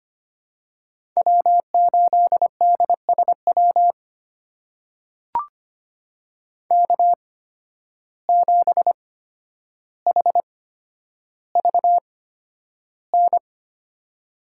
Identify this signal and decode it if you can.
Morse code, heard as a keyed tone, about 25 wpm, spaced as for 11 wpm, 700 Hz. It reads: W8DSW K7HVN
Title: USA Call Signs - 5 Characters - 8x Character Spacing - 25wpm